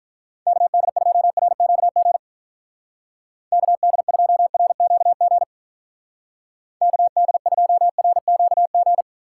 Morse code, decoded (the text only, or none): KD2RQG KD2RQG KD2RQG